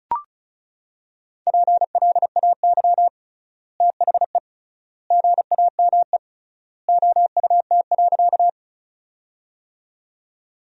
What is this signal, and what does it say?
Morse code, heard as a keyed tone, about 34 wpm, 700 Hz. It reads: PLAY THE GAME OUT.
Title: 3x Word-Spacing - Sentences from Top 500 Words - 35wpm